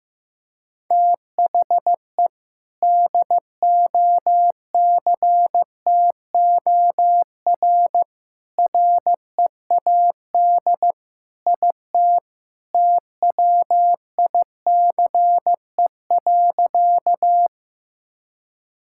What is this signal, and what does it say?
Morse code, heard as a keyed tone, about 15 wpm, 700 Hz. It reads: THE DOCTOR READ IT TWICE.
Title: Sentences from Top 1600 Words 15wpm